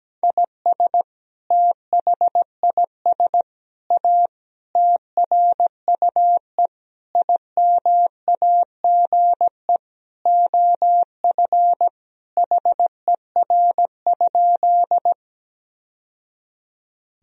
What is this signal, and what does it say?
Morse code, heard as a keyed tone, about 17 wpm, 700 Hz. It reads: IS THIS A TRUE IMAGE OF HER?
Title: Sentences from Top 1000 Words - Review Part 4 - 17wpm